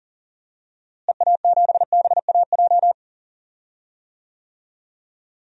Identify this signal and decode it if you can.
Morse code, heard as a keyed tone, about 40 wpm, 700 Hz. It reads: EA7BAJ
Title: DXCC-100 Call Signs - 6 Characters - Repeated 3x - 40wpm